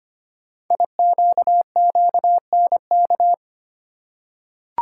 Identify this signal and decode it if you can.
Morse code, heard as a keyed tone, about 25 wpm, 700 Hz. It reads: IQQNK
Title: Code Groups - 5 Letters - 25wpm